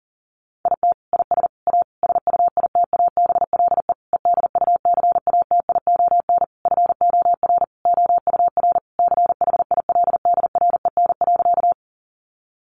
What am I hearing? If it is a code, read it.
Morse code, heard as a keyed tone, about 40 wpm, 700 Hz. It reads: IT IS A SUITABLE EDUCATION FOR OUR CHILDREN.
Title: Sentences from Top 1400 Words 40wpm